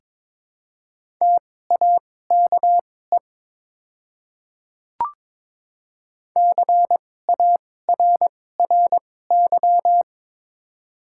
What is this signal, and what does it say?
Morse code, heard as a keyed tone, about 22 wpm, spaced as for 16 wpm, 700 Hz. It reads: TAKE CARRY
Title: 2x Character Spacing - Top 300 Words - 22wpm